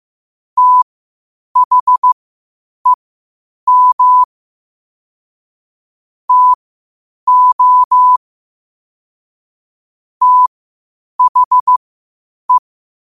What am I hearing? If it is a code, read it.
Morse code, heard as a keyed tone, about 15 wpm, spaced as for 8 wpm, 1000 Hz. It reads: THEM TO THE
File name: qotdfiles / 8wpm